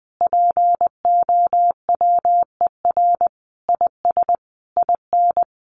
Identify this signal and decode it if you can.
Morse code, heard as a keyed tone, about 20 wpm, 700 Hz. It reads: POWER IS IN
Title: Sentences from Top 900 Words 20wpm